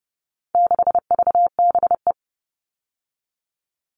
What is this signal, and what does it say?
Morse code, heard as a keyed tone, about 30 wpm, 700 Hz. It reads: 6VBE
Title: All Cut Numbers - 4-Digit 30wpm